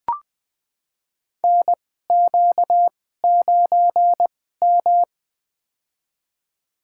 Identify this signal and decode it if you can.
Morse code, heard as a keyed tone, about 20 wpm, spaced as for 14 wpm, 700 Hz. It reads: NQ9M
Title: USA Call Signs - 4 Characters - 2x Character Spacing - 20wpm